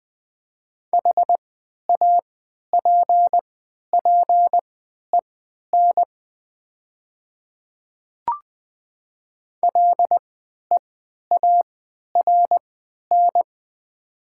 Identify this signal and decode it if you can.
Morse code, heard as a keyed tone, about 20 wpm, spaced as for 11 wpm, 700 Hz. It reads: HAPPEN LEARN
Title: 3x Character Spacing - Top 500 Words - 20wpm